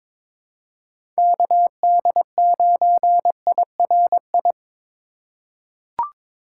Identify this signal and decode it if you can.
Morse code, heard as a keyed tone, about 22 wpm, 700 Hz. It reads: KD9IRI
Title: USA Call Signs - 6 Characters - 2x Character Spacing - 22wpm